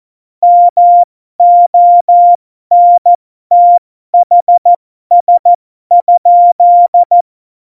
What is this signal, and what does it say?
Morse code, heard as a keyed tone, about 14 wpm, spaced as for 12 wpm, 700 Hz. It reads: MONTHS?
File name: Morsekurs / Witze